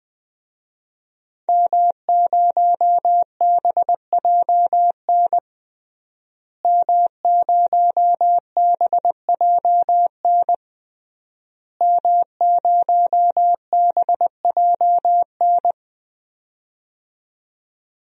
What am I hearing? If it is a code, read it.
Morse code, heard as a keyed tone, about 20 wpm, 700 Hz. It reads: M0BJN M0BJN M0BJN
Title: DXCC-100 Call Signs - 5 Characters - Repeated 3x - 20wpm